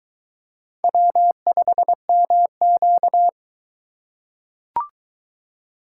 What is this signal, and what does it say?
Morse code, heard as a keyed tone, about 23 wpm, 700 Hz. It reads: W5MQ